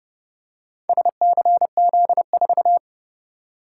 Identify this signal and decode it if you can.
Morse code, heard as a keyed tone, about 29 wpm, 700 Hz. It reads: SCZ4